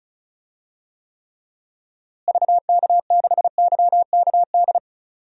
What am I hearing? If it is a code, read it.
Morse code, heard as a keyed tone, about 36 wpm, 700 Hz. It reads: VK6YKD
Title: DXCC-100 Call Signs - 6 Characters - Repeated 2x - 35wpm